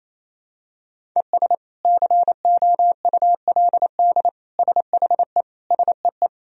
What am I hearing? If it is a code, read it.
Morse code, heard as a keyed tone, about 28 wpm, 700 Hz. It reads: ES COULD SHE SEE
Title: Sentences from Top 100 Words and Abbreviations 28wpm